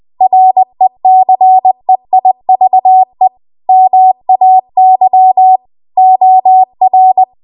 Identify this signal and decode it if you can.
Morse code, heard as a keyed tone, about 20 wpm, 750 Hz. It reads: RECEIVE MAY OR
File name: Archive / 20 WPM / 20WPM